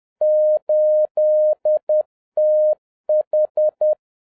Morse code, heard as a keyed tone, about 10 wpm, 610 Hz. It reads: 8TH